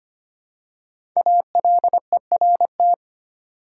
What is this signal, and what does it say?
Morse code, heard as a keyed tone, about 26 wpm, 700 Hz. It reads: ALERT
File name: 5 Letter Words - First 3 Letters Given - 25wpm